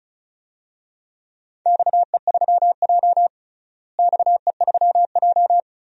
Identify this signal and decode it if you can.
Morse code, heard as a keyed tone, about 36 wpm, 690 Hz. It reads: XE3J XE3J
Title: DXCC-100 Call Signs - 4 Characters - Repeated 2x - 35wpm